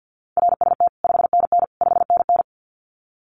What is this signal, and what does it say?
Morse code, heard as a keyed tone, about 50 wpm, 700 Hz. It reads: RST 5NN 5NN
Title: Common QSO Phrases Encore 50wpm